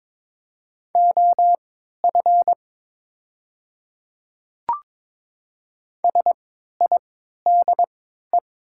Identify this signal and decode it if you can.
Morse code, heard as a keyed tone, about 22 wpm, spaced as for 12 wpm, 700 Hz. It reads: OF SIDE
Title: 3x Character Spacing - Top 100 Words - 22wpm